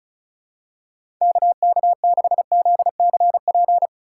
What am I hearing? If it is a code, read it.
Morse code, heard as a keyed tone, about 34 wpm, 700 Hz. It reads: KK6ZCP
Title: USA Call Signs - 6 Characters - 4x Character Spacing - 35wpm